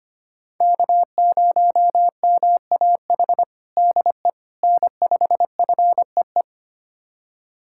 Morse code, heard as a keyed tone, about 25 wpm, 700 Hz. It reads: K0MAH DE N5FEE
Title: USA Callsign DE Callsign 25wpm